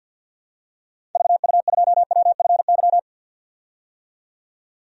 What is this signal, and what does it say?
Morse code, heard as a keyed tone, about 47 wpm, 700 Hz. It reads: VU2WFY
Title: DXCC-100 Call Signs - 6 Characters - 50wpm